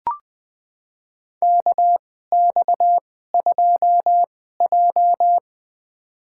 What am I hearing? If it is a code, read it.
Morse code, heard as a keyed tone, about 20 wpm, spaced as for 14 wpm, 700 Hz. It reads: KX2J
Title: USA Call Signs - 4 Characters - 2x Character Spacing - 20wpm